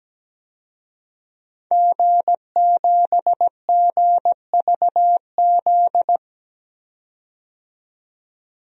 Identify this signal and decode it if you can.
Morse code, heard as a keyed tone, about 17 wpm, 700 Hz. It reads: G7GVZ
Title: DXCC-100 Call Signs - 5 Characters - 17wpm